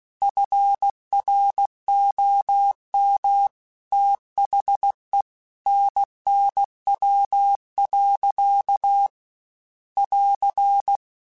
Morse code, heard as a keyed tone, about 16 wpm, 800 Hz. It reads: FROM THE NNW. <AR>